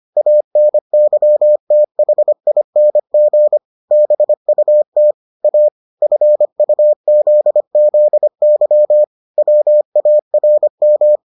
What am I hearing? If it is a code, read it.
Morse code, heard as a keyed tone, about 25 wpm, 600 Hz. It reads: ANYTHING BUT A FUZZY WARM